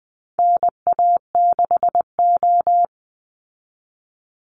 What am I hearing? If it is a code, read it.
Morse code, heard as a keyed tone, about 20 wpm, 700 Hz. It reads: NA6O
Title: Call Signs Repeated 2x - 20wpm